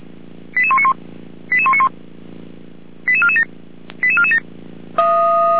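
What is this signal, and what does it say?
Five-tone selective call (ZVEI), heard as quick runs of five tones, tones of about 70 ms. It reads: 88191 88191 88447 88447
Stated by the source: ZVEI five-tone call 88569 received by SDR.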